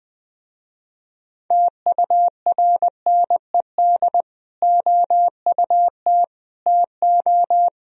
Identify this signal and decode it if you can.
Morse code, heard as a keyed tone, about 20 wpm, 700 Hz. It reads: TURNED OUT TO